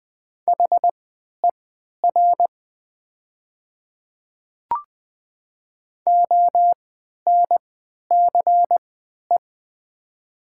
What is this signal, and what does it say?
Morse code, heard as a keyed tone, about 20 wpm, spaced as for 11 wpm, 700 Hz. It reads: HER ONCE
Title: 3x Character Spacing - Top 300 Words - 20wpm